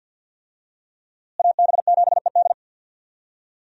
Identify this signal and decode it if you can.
Morse code, heard as a keyed tone, about 48 wpm, 700 Hz. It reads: AB7ED
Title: USA Call Signs - 5 Characters - Repeated 3x - 50wpm